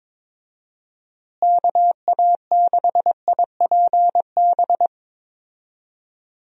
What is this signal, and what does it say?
Morse code, heard as a keyed tone, about 22 wpm, 700 Hz. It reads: KA6IPB
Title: USA Call Signs - 6 Characters - Repeated 2x - 22wpm